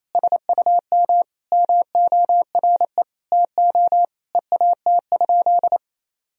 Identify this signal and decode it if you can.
Morse code, heard as a keyed tone, about 28 wpm, 700 Hz. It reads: SUM MORE TO EAT?
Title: Sentences from Top 300 Words and Abbreviations 28wpm